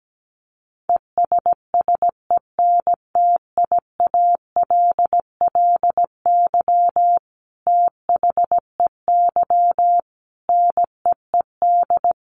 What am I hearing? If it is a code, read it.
Morse code, heard as a keyed tone, about 17 wpm, 700 Hz. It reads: ESSENTIALLY THEY NEED
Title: Sentences from Top 1300 Words 17wpm